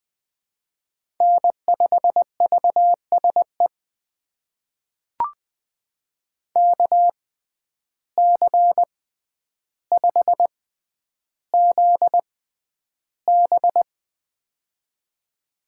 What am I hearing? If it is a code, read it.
Morse code, heard as a keyed tone, about 20 wpm, spaced as for 10 wpm, 700 Hz. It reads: N5VSE KC5ZB
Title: USA Call Signs - 5 Characters - 6x Character Spacing - 20wpm